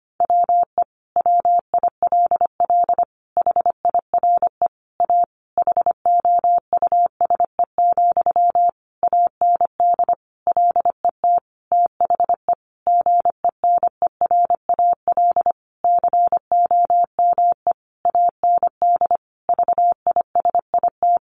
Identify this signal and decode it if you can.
Morse code, heard as a keyed tone, about 25 wpm, 700 Hz. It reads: WE WILL HIRE A HOUSE, AND LET THE GENERAL COME AND VISIT
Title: Sentences from Top 1400 Words 25wpm